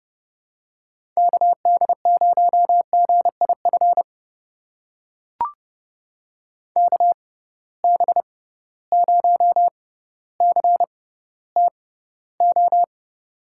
Morse code, heard as a keyed tone, about 30 wpm, spaced as for 15 wpm, 700 Hz. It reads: KD0GIF KB0CTO
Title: USA Call Signs - 6 Characters - 6x Character Spacing - 30wpm